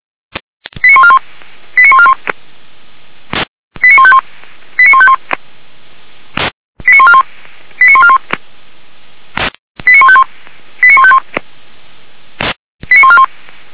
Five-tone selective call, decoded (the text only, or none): 89141 89141 89151 89151 89141 89141 89151 89151 89141